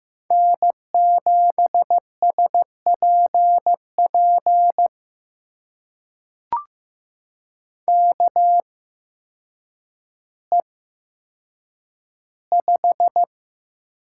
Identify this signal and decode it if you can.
Morse code, heard as a keyed tone, about 15 wpm, spaced as for 8 wpm, 700 Hz. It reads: N7SPP KE5
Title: USA Call Signs - 8x Character Spacing - 15wpm